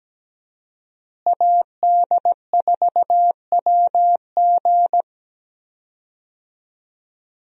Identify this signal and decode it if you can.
Morse code, heard as a keyed tone, about 17 wpm, 700 Hz. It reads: AD4WG
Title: USA Call Signs - 5 Characters - Repeated 2x - 17wpm